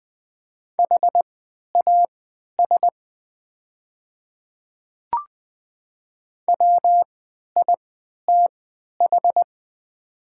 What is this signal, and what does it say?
Morse code, heard as a keyed tone, about 20 wpm, spaced as for 11 wpm, 700 Hz. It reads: HAS WITH
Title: 3x Character Spacing - Top 100 Words - 20wpm